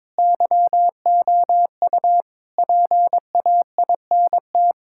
Morse code, heard as a keyed tone, about 22 wpm, 700 Hz. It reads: YOU PAINT
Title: Sentences from Top 1000 Words - Review Part 6 - 22wpm